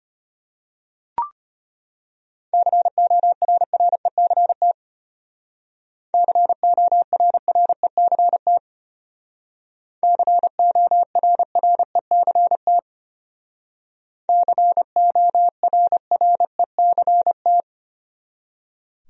Morse code, heard as a keyed tone, about 31 wpm, 700 Hz. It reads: CORRECT CORRECT CORRECT CORRECT